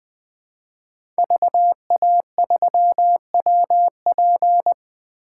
Morse code, heard as a keyed tone, about 20 wpm, 700 Hz. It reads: VA3WP
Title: 2023 POTA Activator Call Signs - 20wpm